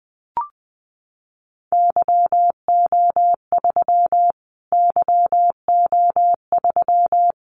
Morse code, heard as a keyed tone, about 20 wpm, 700 Hz. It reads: YO3 YO3
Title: Callsign Prefixes for Top 25 DXCC - Repeated 3x - 20wpm